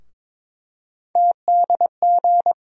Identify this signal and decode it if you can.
Morse code, heard as a keyed tone, about 22 wpm, 700 Hz. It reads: TDG
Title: USA Call Sign Postfixes - 3 Letters 22wpm